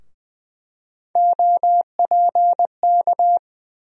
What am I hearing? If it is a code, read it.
Morse code, heard as a keyed tone, about 21 wpm, 700 Hz. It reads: OPK
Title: USA Call Sign Postfixes - 3 Letters 20wpm